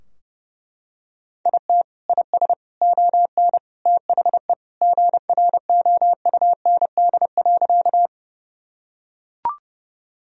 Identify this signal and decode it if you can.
Morse code, heard as a keyed tone, about 32 wpm, 700 Hz. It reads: IT IS ON THE GROUND.
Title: Sentences from Top 1000 Words - Review Part 2 - 30wpm